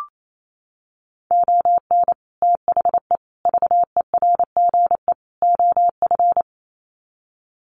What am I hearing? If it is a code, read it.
Morse code, heard as a keyed tone, about 28 wpm, 700 Hz. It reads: ON THE VERGE OF
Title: Sets of 4 Words 28wpm